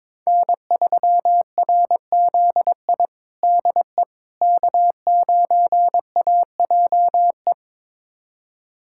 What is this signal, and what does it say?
Morse code, heard as a keyed tone, about 22 wpm, 700 Hz. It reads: N3RZI DE K9AJE
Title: USA Callsign DE Callsign - 5 Characters - 22wpm